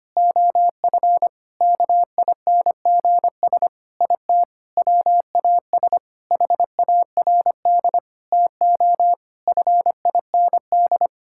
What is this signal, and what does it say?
Morse code, heard as a keyed tone, about 25 wpm, 700 Hz. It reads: OF KINGS IT WAS HARD TO FIND
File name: practice / 25wpm / Sentences with 3-Word Phrases - 25wpm